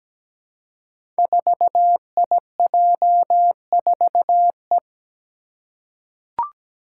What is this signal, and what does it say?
Morse code, heard as a keyed tone, about 17 wpm, 700 Hz. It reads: 4IJ4E
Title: Five Challenging Letter-Number Combinations 17wpm